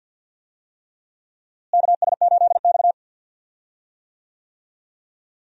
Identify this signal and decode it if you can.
Morse code, heard as a keyed tone, about 46 wpm, 700 Hz. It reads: KS8X
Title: USA Call Signs - 4 Characters - Repeated 3x - 50wpm